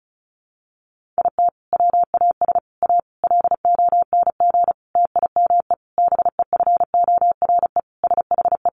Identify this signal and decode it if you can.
Morse code, heard as a keyed tone, about 34 wpm, 700 Hz. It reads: IT WAS A LONG TIME BEFORE SHE
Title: Sentences from Top 500 Words 35wpm